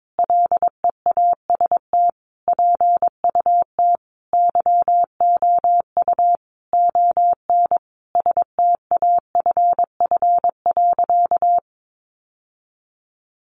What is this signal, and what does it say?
Morse code, heard as a keyed tone, about 22 wpm, 700 Hz. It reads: LEAST PUT YOU ON STAFF.
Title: Sentences from Top 1200 Words 22wpm